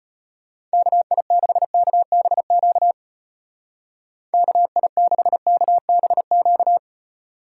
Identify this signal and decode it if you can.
Morse code, heard as a keyed tone, about 31 wpm, 700 Hz. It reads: KI6KBQ KI6KBQ